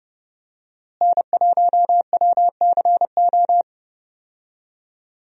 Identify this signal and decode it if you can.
Morse code, heard as a keyed tone, about 30 wpm, 700 Hz. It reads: N1WCO
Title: USA Call Signs - 5 Characters - Repeated 3x - 30wpm